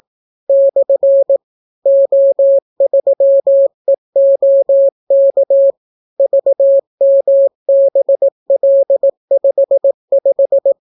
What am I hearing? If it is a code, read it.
Morse code, heard as a keyed tone, about 18 wpm, 550 Hz. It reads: / O3EOK VMBL55